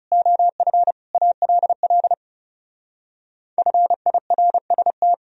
Morse code, heard as a keyed tone, about 38 wpm, 700 Hz. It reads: OF ALL FIRST